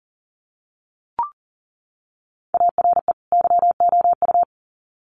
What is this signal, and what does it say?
Morse code, heard as a keyed tone, about 40 wpm, 700 Hz. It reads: ARE YOU
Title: Sets of 2 Words 40wpm